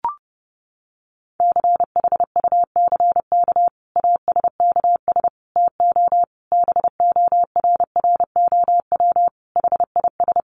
Morse code, heard as a keyed tone, about 28 wpm, 700 Hz. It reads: CHUCK ASKS TO BORROW HIS
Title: Additional Easy Phrases 30wpm